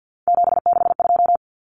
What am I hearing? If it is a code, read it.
Morse code, heard as a keyed tone, about 50 wpm, 700 Hz. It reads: 762
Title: Common Cut Numbers - 3-Digit 50wpm